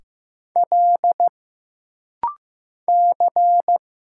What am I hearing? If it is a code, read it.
Morse code, heard as a keyed tone, about 15 wpm, 700 Hz. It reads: L C
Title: Single Letters - Rapid-Fire 15wpm